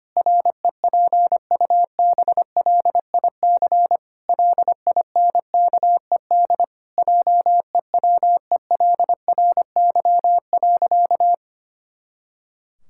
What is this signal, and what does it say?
Morse code, heard as a keyed tone, about 25 wpm, 700 Hz. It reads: REPUBLIC LINKED JEWELRY.
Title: Pangram Challenge v2 - 25wpm